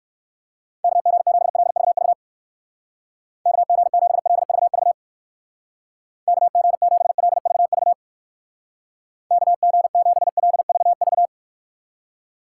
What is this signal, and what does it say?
Morse code, heard as a keyed tone, about 56 wpm, 700 Hz. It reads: KG7LVV KG7LVV KG7LVV KG7LVV
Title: Speed Racing - USA 2x3 Call Signs 45wpm